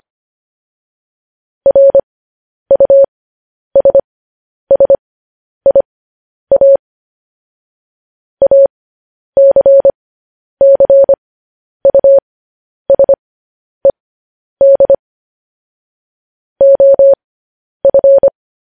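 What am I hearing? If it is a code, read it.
Morse code, heard as a keyed tone, about 25 wpm, spaced as for 10 wpm, 550 Hz. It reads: RUSSIA ACCUSED OF
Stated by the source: Morse code news headlines.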